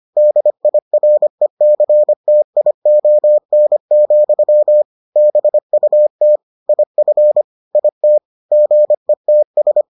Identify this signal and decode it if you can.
Morse code, heard as a keyed tone, about 25 wpm, 600 Hz. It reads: DIRECTION, BUT IF IT GETS